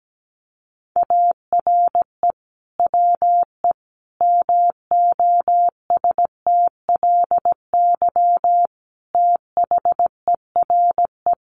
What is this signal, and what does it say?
Morse code, heard as a keyed tone, about 17 wpm, 700 Hz. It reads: ARE WE MOSTLY THERE
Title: Sentences from Top 1300 Words 17wpm